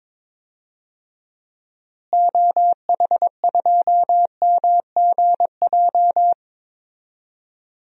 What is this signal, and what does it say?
Morse code, heard as a keyed tone, about 22 wpm, 700 Hz. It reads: OH2MGJ
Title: DXCC-100 Call Signs - 6 Characters - Repeated 3x - 22wpm